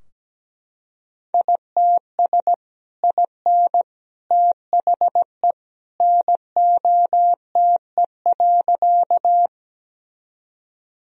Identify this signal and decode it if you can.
Morse code, heard as a keyed tone, about 17 wpm, 700 Hz. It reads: ITS IN THE NOTE.
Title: Sentences from Top 1000 Words - Review Part 2 - 17wpm